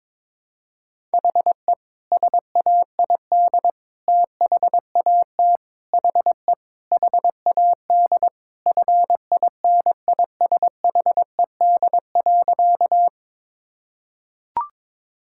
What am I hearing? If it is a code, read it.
Morse code, heard as a keyed tone, about 23 wpm, 700 Hz. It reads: HE SAID THAT HE HAD FINISHED.
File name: practice / 22wpm / Sentences with 3-Word Phrases - 22wpm